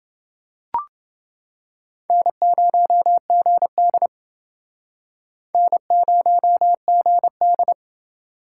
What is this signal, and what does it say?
Morse code, heard as a keyed tone, about 28 wpm, 700 Hz. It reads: N0GD N0GD